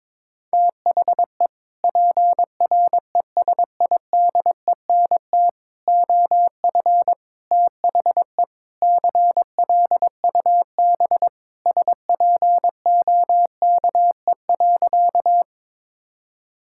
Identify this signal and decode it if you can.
Morse code, heard as a keyed tone, about 22 wpm, 700 Hz. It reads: THE PRESIDENT OF THE CLUB SPOKE.
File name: practice / 22wpm / Sentences with 4-Word Phrases - 22wpm